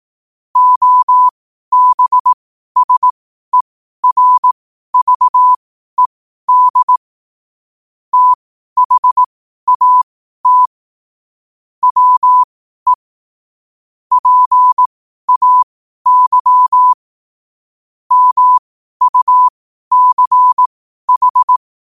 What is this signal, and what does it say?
Morse code, heard as a keyed tone, about 18 wpm, spaced as for 12 wpm, 1000 Hz. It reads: OBSERVED THAT WE PAY MUCH
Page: Quotes for Wed, 13 Aug 2025 in Morse Code at 12 words per minute.